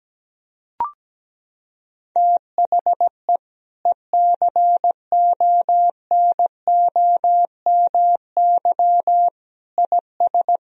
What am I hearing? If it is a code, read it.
Morse code, heard as a keyed tone, about 17 wpm, 700 Hz. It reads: THE ECONOMY IS
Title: Sentences from Top 1000 Words - Review Part 3 - 17wpm